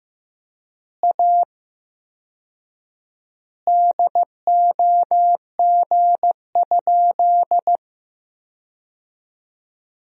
Morse code, heard as a keyed tone, about 15 wpm, 700 Hz. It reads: A DOG?
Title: 4x Word-Spacing - Sentences from Top 400 Words - 15wpm